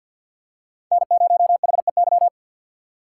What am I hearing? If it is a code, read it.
Morse code, heard as a keyed tone, about 46 wpm, 700 Hz. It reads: N0HEY
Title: USA Call Signs - 5 Characters - 4x Character Spacing - 50wpm